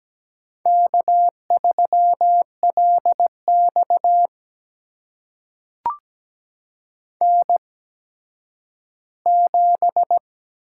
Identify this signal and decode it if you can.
Morse code, heard as a keyed tone, about 17 wpm, spaced as for 10 wpm, 700 Hz. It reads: K3LX N7
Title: USA Call Signs - 4 Characters - 8x Character Spacing - 17wpm